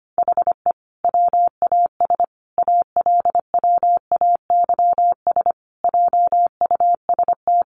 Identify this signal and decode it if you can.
Morse code, heard as a keyed tone, about 25 wpm, 700 Hz. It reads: HE WAS ALWAYS JUST
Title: Sentences from Top 600 Words 25wpm